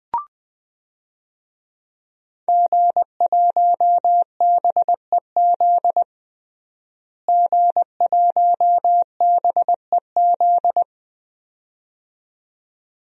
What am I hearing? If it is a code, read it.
Morse code, heard as a keyed tone, about 20 wpm, 700 Hz. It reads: G1BEZ G1BEZ
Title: DXCC-100 Call Signs - 5 Characters - Repeated 2x - 20wpm